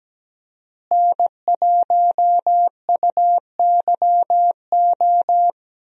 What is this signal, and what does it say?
Morse code, heard as a keyed tone, about 17 wpm, 700 Hz. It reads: N1UYO
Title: USA Call Signs - 5 Characters - Repeated 3x - 17wpm